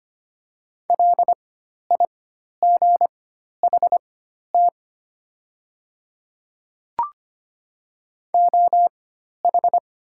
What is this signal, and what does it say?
Morse code, heard as a keyed tone, about 25 wpm, spaced as for 12 wpm, 700 Hz. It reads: LIGHT OH
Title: 4x Character Spacing - Top 500 Words - 25wpm